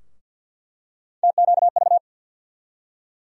Morse code, heard as a keyed tone, about 52 wpm, 690 Hz. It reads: TQV
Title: USA Call Sign Postfixes - 3 Letters 50wpm